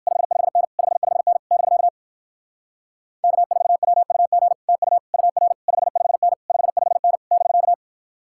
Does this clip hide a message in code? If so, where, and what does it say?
Yes: Morse code from 0.1 to 7.7 s, heard as a keyed tone, about 69 wpm, 700 Hz. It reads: 55N 55N <BK> K4WUG TU UR 55N 55N <BK>